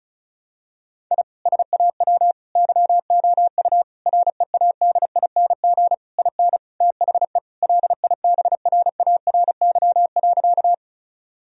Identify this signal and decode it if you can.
Morse code, heard as a keyed tone, about 35 wpm, 700 Hz. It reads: I SAW YOU READING IN THE LIBRARY.
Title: Sentences from Top 1000 Words - Review Part 4 - 35wpm